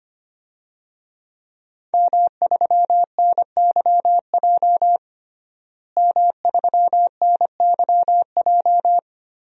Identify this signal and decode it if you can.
Morse code, heard as a keyed tone, about 25 wpm, 700 Hz. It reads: M3NYJ M3NYJ
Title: DXCC-100 Call Signs - 5 Characters - Repeated 2x - 25wpm